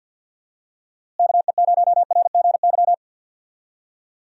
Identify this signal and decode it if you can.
Morse code, heard as a keyed tone, about 52 wpm, 700 Hz. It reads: KE0RGY
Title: USA Call Signs - 6 Characters - Repeated 2x - 50wpm